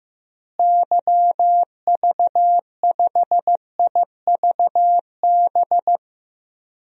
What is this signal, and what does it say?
Morse code, heard as a keyed tone, about 15 wpm, 700 Hz. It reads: YV5IVB
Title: DXCC-100 Call Signs - 6 Characters - Repeated 3x - 15wpm